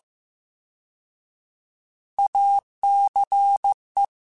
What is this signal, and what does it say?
Morse code, heard as a keyed tone, about 15 wpm, 800 Hz. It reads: ACE
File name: Alphabetical / Alphabetical mp3s / As @13